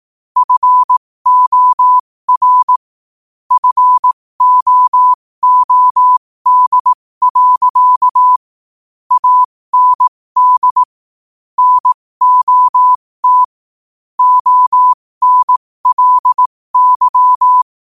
Morse code, heard as a keyed tone, about 18 wpm, spaced as for 15 wpm, 1000 Hz. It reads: FOR FOOD. AND NOT ONLY
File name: qotdfiles / 15wpm